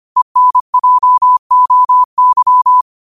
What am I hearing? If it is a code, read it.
Morse code, heard as a keyed tone, about 25 wpm, 1000 Hz. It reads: ENJOY
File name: qotdfiles / 25wpm